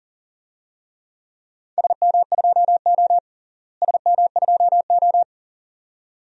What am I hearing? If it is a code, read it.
Morse code, heard as a keyed tone, about 40 wpm, 700 Hz. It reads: SM2O SM2O
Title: DXCC-100 Call Signs - 4 Characters - Repeated 2x - 40wpm